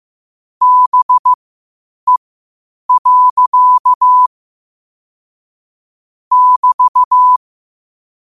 Morse code, heard as a keyed tone, about 15 wpm, spaced as for 8 wpm, 1000 Hz. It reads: BE. <BT>
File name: 8wpm